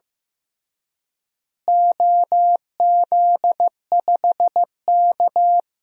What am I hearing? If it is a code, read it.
Morse code, heard as a keyed tone, about 15 wpm, 700 Hz. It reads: OZ5K
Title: DXCC-100 Call Signs - 4 Characters - Repeated 2x - 15wpm